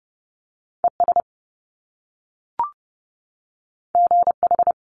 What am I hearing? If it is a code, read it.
Morse code, heard as a keyed tone, about 30 wpm, 700 Hz. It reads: ES GH
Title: Top Two Letter Combinations in Words 30wpm